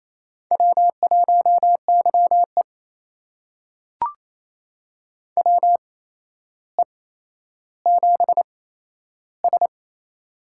Morse code, heard as a keyed tone, about 28 wpm, spaced as for 12 wpm, 700 Hz. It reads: W1YE WE7S
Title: USA Call Signs - 4 Characters - 8x Character Spacing - 28wpm